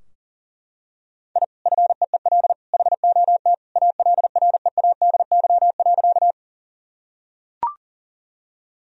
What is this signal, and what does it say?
Morse code, heard as a keyed tone, about 40 wpm, 700 Hz. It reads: I FEEL HOT ALREADY.
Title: Sentences from Top 1000 Words - Review Part 3 - 40wpm